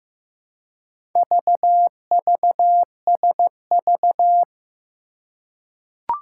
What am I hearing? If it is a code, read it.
Morse code, heard as a keyed tone, about 15 wpm, 700 Hz. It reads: VVSV
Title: Four Challenging Letter-Number Combinations 15wpm